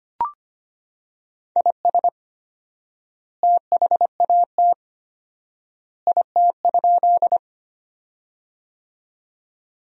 Morse code, heard as a keyed tone, about 25 wpm, 700 Hz. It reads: IS THAT IT?
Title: 4x Word-Spacing - Sentences from Top 100 Words - 25wpm